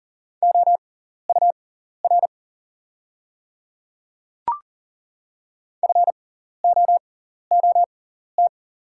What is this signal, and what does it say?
Morse code, heard as a keyed tone, about 40 wpm, spaced as for 14 wpm, 700 Hz. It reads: OUR FOOT
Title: 6x Character Spacing - Top 500 Words - 40wpm